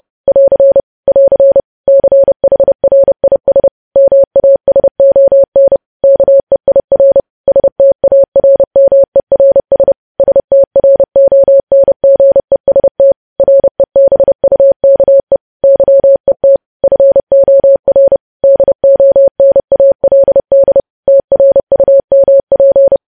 Morse code, heard as a keyed tone, about 30 wpm, 550 Hz. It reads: <AR> <AR> CHRIS MASON KEIR STARMERS STRONGEST REBUKE YET FOR DONALD TRUMP